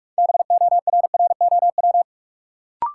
Morse code, heard as a keyed tone, about 44 wpm, 700 Hz. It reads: BORROW
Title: Sentences from Top 1500 Words - New Word Drill Alt - 45wpm